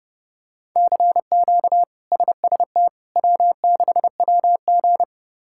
Morse code, heard as a keyed tone, about 30 wpm, 700 Hz. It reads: CQ SST W6WG
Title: K1USN SST QSO Practice - 30wpm